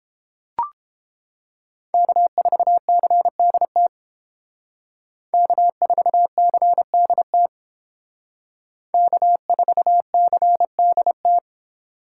Morse code, heard as a keyed tone, about 29 wpm, 700 Hz. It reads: K4CDT K4CDT K4CDT